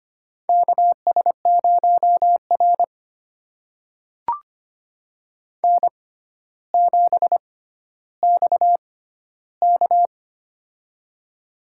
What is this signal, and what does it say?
Morse code, heard as a keyed tone, about 25 wpm, spaced as for 13 wpm, 700 Hz. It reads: KS0R N7XK
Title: USA Call Signs - 6x Character Spacing - 25wpm